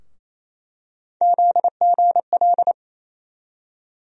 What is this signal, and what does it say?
Morse code, heard as a keyed tone, about 28 wpm, 700 Hz. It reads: ZGL
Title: USA Call Sign Postfixes - 3 Letters 28wpm